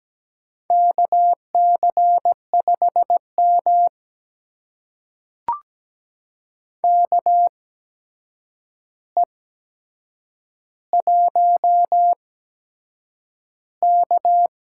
Morse code, heard as a keyed tone, about 17 wpm, spaced as for 7 wpm, 700 Hz. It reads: KC5M KE1K
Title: USA Call Signs - 4 Characters - 8x Character Spacing - 17wpm